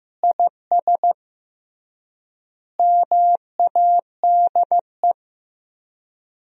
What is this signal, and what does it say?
Morse code, heard as a keyed tone, about 15 wpm, 700 Hz. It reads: IS MADE
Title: 3x Word-Spacing - Sentences from Top 500 Words - 15wpm